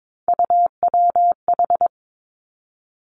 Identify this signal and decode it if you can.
Morse code, heard as a keyed tone, about 22 wpm, 700 Hz. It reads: UWH
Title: Three Challenging Letter-Number Combinations 22wpm